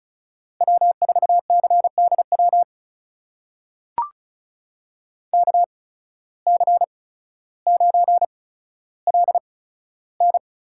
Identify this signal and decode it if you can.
Morse code, heard as a keyed tone, about 36 wpm, spaced as for 15 wpm, 700 Hz. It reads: W4CDW KC9LN
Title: USA Call Signs - 5 Characters - 8x Character Spacing - 35wpm